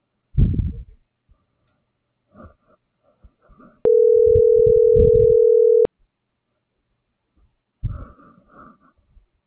Ringback tone (North America), 440 + 480 Hz.
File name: CENTRAL MABE / out celeste.alcantara